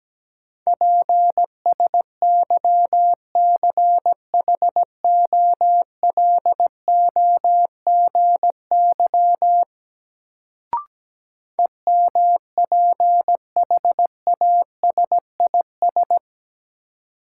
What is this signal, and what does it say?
Morse code, heard as a keyed tone, about 17 wpm, 700 Hz. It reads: PSYCHOLOGY EMPHASIS
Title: Sentences from Top 1400 Words - New Word Drill Alt - 17wpm